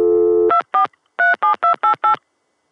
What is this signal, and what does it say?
DTMF, heard as telephone touch tones, tones of about 110 ms. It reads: 2437288